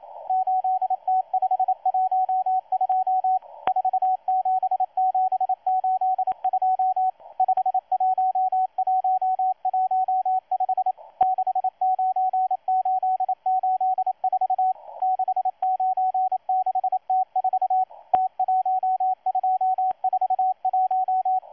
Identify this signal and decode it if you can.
Morse code, heard as a keyed tone, about 27 wpm, 750 Hz, with radio noise. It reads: 8T512 47782 51115 69884 696T4 T1241
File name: M12-2016-07-07-1310utc-13926khz-error-restart